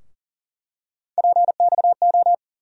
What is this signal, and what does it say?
Morse code, heard as a keyed tone, about 40 wpm, 700 Hz. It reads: PXO
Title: USA Call Sign Postfixes - 3 Letters 40wpm